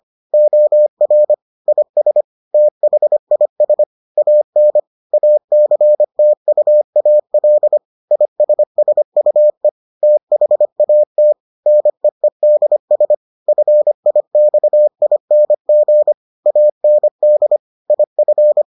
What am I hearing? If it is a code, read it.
Morse code, heard as a keyed tone, about 25 wpm, 600 Hz. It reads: OR IS THIS AN ACTUAL ISSUE THAT NEEDS FIXING AND IF